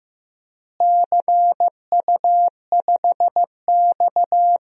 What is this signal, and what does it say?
Morse code, heard as a keyed tone, about 15 wpm, 700 Hz. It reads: CU5X
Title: DXCC-100 Call Signs - 4 Characters - Repeated 2x - 15wpm